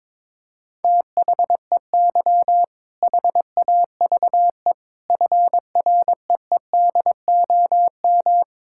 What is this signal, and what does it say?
Morse code, heard as a keyed tone, about 22 wpm, 700 Hz. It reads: THEY HAVE FREEDOM
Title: Sentences from Top 1000 Words - Review Part 5 - 22wpm